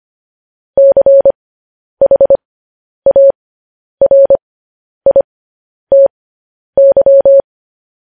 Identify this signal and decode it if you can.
Morse code, heard as a keyed tone, about 25 wpm, spaced as for 10 wpm, 550 Hz. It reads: CHARITY
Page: news-headlines-in-morse-code-at-10-wpm